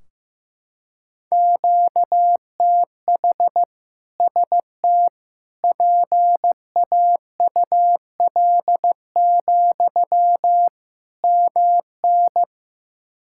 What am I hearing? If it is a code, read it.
Morse code, heard as a keyed tone, about 15 wpm, 700 Hz. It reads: QTH ST PAUL, MN
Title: K1USN SST QSO Practice 15wpm